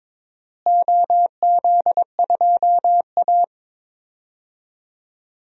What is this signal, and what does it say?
Morse code, heard as a keyed tone, about 22 wpm, 700 Hz. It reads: OZ2A
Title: DXCC-100 Call Signs - 4 Characters - Repeated 3x - 22wpm